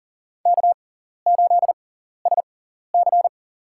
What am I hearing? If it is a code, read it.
Morse code, heard as a keyed tone, about 37 wpm, spaced as for 14 wpm, 700 Hz. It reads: K8SC
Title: USA Call Signs - 6x Character Spacing - 40wpm